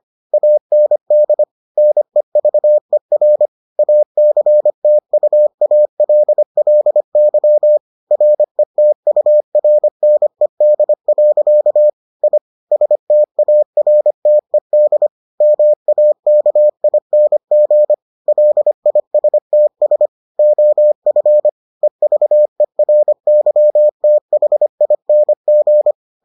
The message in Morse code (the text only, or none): AND NEVER ACTUALLY RETURNED. I STARTED MAKING LISTS OF EVERYTHING